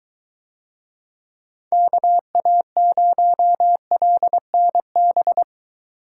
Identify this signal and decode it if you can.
Morse code, heard as a keyed tone, about 22 wpm, 700 Hz. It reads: KA0LNB